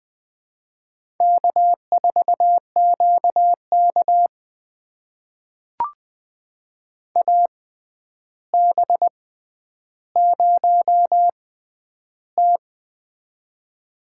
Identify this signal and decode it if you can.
Morse code, heard as a keyed tone, about 20 wpm, spaced as for 10 wpm, 700 Hz. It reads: K4QK AB0T
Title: USA Call Signs - 4 Characters - 6x Character Spacing - 20wpm